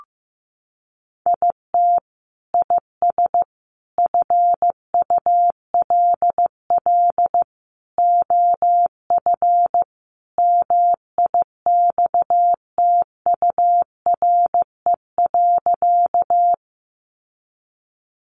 Morse code, heard as a keyed tone, about 15 wpm, 700 Hz. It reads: IT IS FULL OF MIXTURE.
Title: Sentences from Top 1500 Words 15wpm